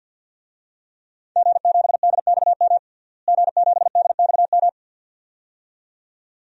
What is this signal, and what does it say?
Morse code, heard as a keyed tone, about 52 wpm, 700 Hz. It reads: G7DXM G7DXM
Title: DXCC-100 Call Signs - 5 Characters - Repeated 2x - 50wpm